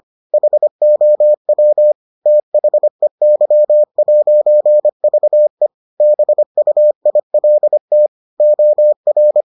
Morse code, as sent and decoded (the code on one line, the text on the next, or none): .... --- .-- / - .... -.-- .----. ...- . / -... ..- .. .-.. - / --- .-.
HOW THEY'VE BUILT OR